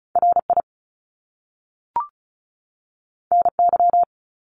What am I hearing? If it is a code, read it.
Morse code, heard as a keyed tone, about 36 wpm, 700 Hz. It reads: RI NY
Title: US States 35wpm